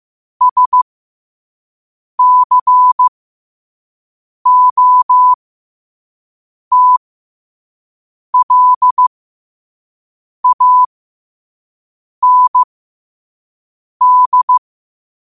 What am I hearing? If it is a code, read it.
Morse code, heard as a keyed tone, about 15 wpm, spaced as for 5 wpm, 1000 Hz. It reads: SCOTLAND
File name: qotdfiles / 5wpm